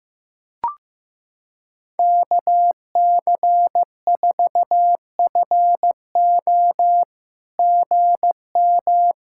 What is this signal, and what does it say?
Morse code, heard as a keyed tone, about 15 wpm, 700 Hz. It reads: KC4FO GM
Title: SOTA QSO Practice 15wpm